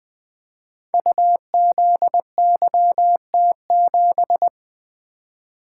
Morse code, heard as a keyed tone, about 20 wpm, 700 Hz. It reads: UZYT7
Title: Code Groups - 5 Letter:Numbers - 20wpm